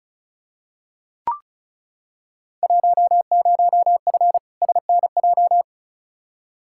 Morse code, heard as a keyed tone, about 36 wpm, 690 Hz. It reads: 10F SNJ
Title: Field Day Exchanges Encore - 35wpm